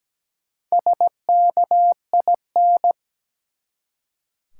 Morse code, heard as a keyed tone, about 17 wpm, 700 Hz. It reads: SKIN
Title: Four Letter Words - 17wpm